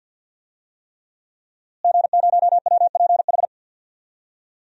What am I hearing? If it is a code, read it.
Morse code, heard as a keyed tone, about 51 wpm, 700 Hz. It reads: G0WPH
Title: DXCC-100 Call Signs - 5 Characters - Repeated 3x - 50wpm